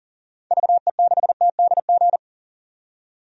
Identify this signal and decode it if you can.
Morse code, heard as a keyed tone, about 40 wpm, 700 Hz. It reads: VE6TDG
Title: DXCC-100 Call Signs - 6 Characters - 40wpm